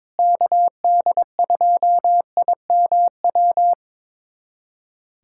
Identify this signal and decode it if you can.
Morse code, heard as a keyed tone, about 22 wpm, 700 Hz. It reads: KD2IMW
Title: USA Callsign DE Callsign 22wpm